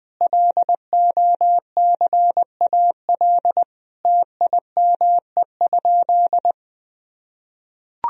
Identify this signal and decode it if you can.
Morse code, heard as a keyed tone, about 21 wpm, 700 Hz. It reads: LOCAL TIME?
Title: Sentences from Top 1000 Words - Review Part 3 - 20wpm